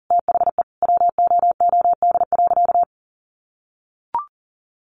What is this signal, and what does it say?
Morse code, heard as a keyed tone, about 40 wpm, 700 Hz. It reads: THE WOOD.
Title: Sentences from Top 1100 Words 40wpm